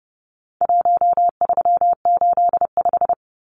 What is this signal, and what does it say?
Morse code, heard as a keyed tone, about 30 wpm, 700 Hz. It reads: 1385
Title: Numbers - 4-Digit 30wpm